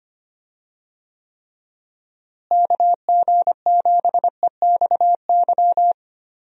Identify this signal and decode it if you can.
Morse code, heard as a keyed tone, about 25 wpm, 700 Hz. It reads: KG7EXY
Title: USA Call Signs - 6 Characters - Repeated 2x - 25wpm